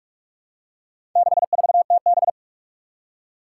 Morse code, heard as a keyed tone, about 42 wpm, 700 Hz. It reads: 64TB
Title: Four Challenging Letter-Number Combinations 45wpm